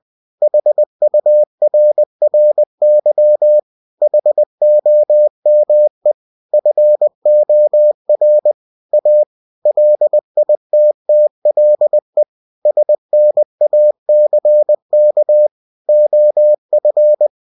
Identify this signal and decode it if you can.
Morse code, heard as a keyed tone, about 20 wpm, 600 Hz. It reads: HURRY HOME FOR A LITTLE SNACK OF
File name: cw-books / winnie-the-pooh / winnie-20wpm-page-0032